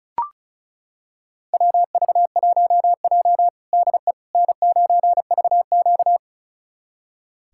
Morse code, heard as a keyed tone, about 34 wpm, 700 Hz. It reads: WV1J DE N9VQ
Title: USA Callsign DE Callsign - 4 Characters - 35wpm